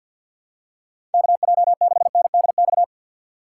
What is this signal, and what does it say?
Morse code, heard as a keyed tone, about 51 wpm, 700 Hz. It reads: KJ6NDX
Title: USA Call Signs - 4x Character Spacing - 50wpm